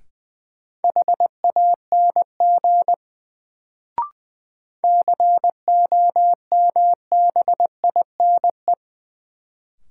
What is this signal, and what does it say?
Morse code, heard as a keyed tone, about 20 wpm, 700 Hz. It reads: HANG COMBINE
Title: Sentences from Top 1600 Words - New Word Drill - 20wpm